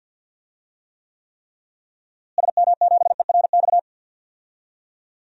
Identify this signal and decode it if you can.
Morse code, heard as a keyed tone, about 51 wpm, 700 Hz. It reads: SM7ERX
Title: DXCC-100 Call Signs - 6 Characters - Repeated 3x - 50wpm